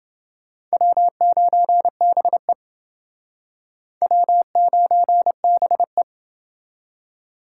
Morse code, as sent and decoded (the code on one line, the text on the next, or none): .-- ----. -... / .-- ----. -... .
W9BE W9BE